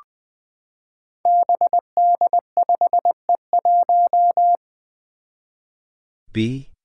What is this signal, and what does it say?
Morse code, heard as a keyed tone, about 20 wpm, 700 Hz. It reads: BD5E1
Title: Five Challenging Letter-Number Combinations 20wpm